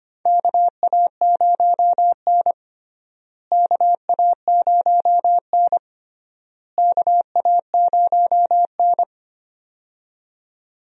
Morse code, heard as a keyed tone, about 25 wpm, 700 Hz. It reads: KA0N KA0N KA0N
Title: USA Call Signs - 4 Characters - Repeated 3x - 25wpm